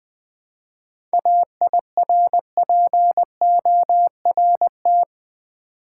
Morse code, heard as a keyed tone, about 20 wpm, 690 Hz. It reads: AIRPORT
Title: 7 Letter Words - First 3 Letters Given - 20wpm